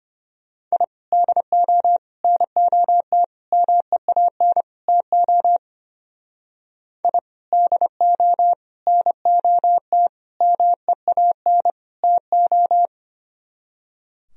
Morse code, heard as a keyed tone, about 32 wpm, 700 Hz. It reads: I DO NOT MEAN TO I DO NOT MEAN TO